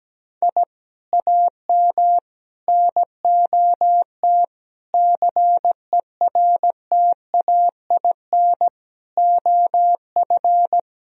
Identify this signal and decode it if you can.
Morse code, heard as a keyed tone, about 17 wpm, 700 Hz. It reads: I AM NOT CERTAIN OF
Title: Sentences from Top 1000 Words - Review Part 2 - 17wpm